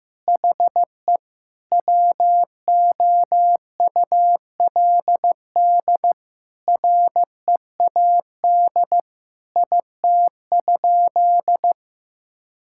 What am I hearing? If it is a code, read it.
Morse code, heard as a keyed tone, about 14 wpm, 700 Hz. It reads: HE WOULD READ IT?
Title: Sentences from Top 1600 Words 15wpm